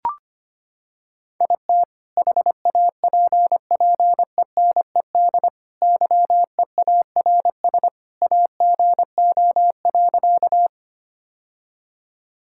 Morse code, heard as a keyed tone, about 25 wpm, 700 Hz. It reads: IT HAPPENED YEARS AGO.
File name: practice / 25wpm / Sentences with 2-Word Phrases - 25wpm